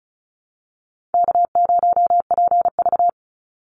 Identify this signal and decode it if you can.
Morse code, heard as a keyed tone, about 34 wpm, 700 Hz. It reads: K0PV
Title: Call Signs Repeated 2x - 35wpm